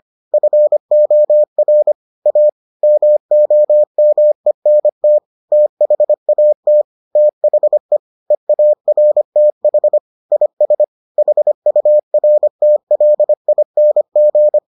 Morse code, heard as a keyed tone, about 25 wpm, 600 Hz. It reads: FOR A MOMENT THAT THE EARTH IS HURTLING